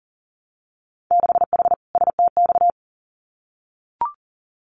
Morse code, heard as a keyed tone, about 37 wpm, 700 Hz. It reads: 6H STX
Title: Winter Field Day Exchanges Encore 40wpm